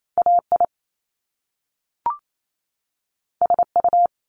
Morse code, heard as a keyed tone, about 28 wpm, 700 Hz. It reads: AI SU
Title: Top Two Letter Combinations in Words 28wpm